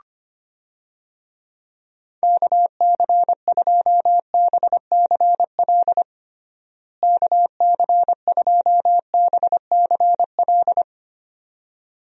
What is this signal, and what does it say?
Morse code, heard as a keyed tone, about 25 wpm, 700 Hz. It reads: KC2BCL KC2BCL
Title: USA Call Signs - 6 Characters - Repeated 2x - 25wpm